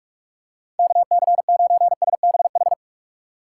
Morse code, heard as a keyed tone, about 46 wpm, 700 Hz. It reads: KC9SBH
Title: USA Call Signs - 6 Characters - 3x Character Spacing - 45wpm